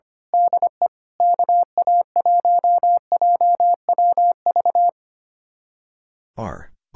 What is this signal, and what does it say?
Morse code, heard as a keyed tone, about 26 wpm, 700 Hz. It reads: DE KA1JWV
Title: Speed-Racing Common QSO Phrases Encore - 25wpm